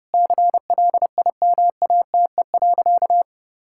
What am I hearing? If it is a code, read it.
Morse code, heard as a keyed tone, about 30 wpm, 700 Hz. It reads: CLIMATE.
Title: Sentences from Top 1600 Words 30wpm